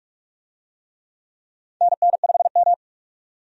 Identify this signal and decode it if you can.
Morse code, heard as a keyed tone, about 46 wpm, 700 Hz. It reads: NN5M
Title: USA Call Signs - 4 Characters - 45wpm